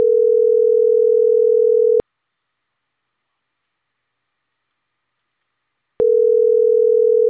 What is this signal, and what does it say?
Ringback tone (North America), 440 + 480 Hz.